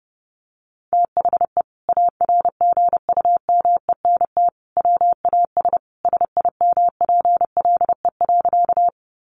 Morse code, heard as a keyed tone, about 28 wpm, 700 Hz. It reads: THE ARGUMENT WAS SIMPLE.
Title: Sentences from Top 1400 Words 30wpm